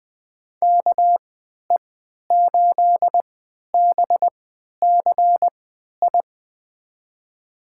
Morse code, heard as a keyed tone, about 20 wpm, spaced as for 11 wpm, 700 Hz. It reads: KE8BCI
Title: USA Call Signs - 6 Characters - 3x Character Spacing - 20wpm